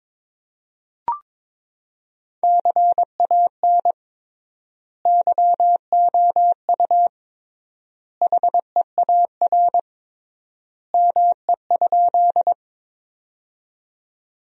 Morse code, heard as a keyed tone, about 21 wpm, 700 Hz. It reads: CAN YOU HEAR ME?
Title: 3x Word-Spacing - Sentences from Top 300 Words - 22wpm